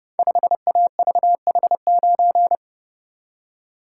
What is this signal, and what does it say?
Morse code, heard as a keyed tone, about 30 wpm, 700 Hz. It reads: 5AVH9
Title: Code Groups - 5 Letter:Numbers - 30wpm